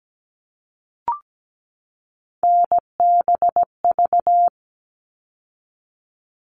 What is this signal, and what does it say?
Morse code, heard as a keyed tone, about 17 wpm, 700 Hz. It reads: NBV
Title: All Cut Numbers - 3-Digit 17wpm